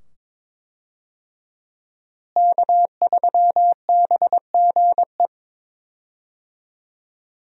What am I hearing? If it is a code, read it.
Morse code, heard as a keyed tone, about 21 wpm, 700 Hz. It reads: K3BGE
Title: USA Call Signs - 5 Characters - Repeated 2x - 22wpm